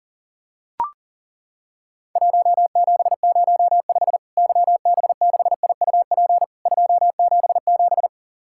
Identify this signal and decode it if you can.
Morse code, heard as a keyed tone, about 40 wpm, 700 Hz. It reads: 1705 YB6IUP 277
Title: Worked All Europe DX Contest - QTC Exchange - 40wpm